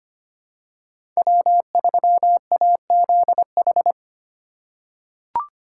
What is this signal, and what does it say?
Morse code, heard as a keyed tone, about 25 wpm, 700 Hz. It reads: W3AZH
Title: USA Call Signs - 5 Characters - 6x Character Spacing - 25wpm